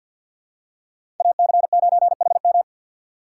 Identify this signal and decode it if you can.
Morse code, heard as a keyed tone, about 53 wpm, 700 Hz. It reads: AC9HM
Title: USA Call Signs - 5 Characters - 2x Character Spacing - 50wpm